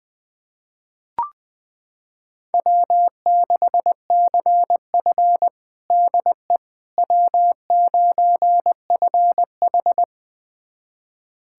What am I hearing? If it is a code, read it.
Morse code, heard as a keyed tone, about 20 wpm, 700 Hz. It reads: W6CF DE W9FH
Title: USA Callsign DE Callsign - 4 Characters - 20wpm